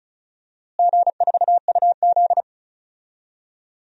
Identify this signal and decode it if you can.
Morse code, heard as a keyed tone, about 34 wpm, 700 Hz. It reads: G4UZ
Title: DXCC-100 Call Signs - 4 Characters - Repeated 2x - 35wpm